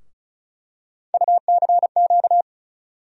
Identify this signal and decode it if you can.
Morse code, heard as a keyed tone, about 36 wpm, 700 Hz. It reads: UCQ
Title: USA Call Sign Postfixes - 3 Letters 35wpm